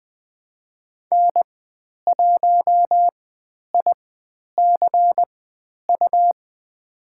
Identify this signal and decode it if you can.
Morse code, heard as a keyed tone, about 20 wpm, spaced as for 10 wpm, 700 Hz. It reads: N1ICU
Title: 102 Lesson 17g - Letter u - Call Signs 20wpm